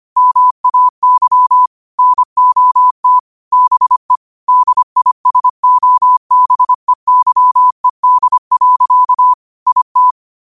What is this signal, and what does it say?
Morse code, heard as a keyed tone, about 25 wpm, 1000 Hz. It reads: MAY NOT BE DISOBEYED. IT
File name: qotdfiles / 25wpm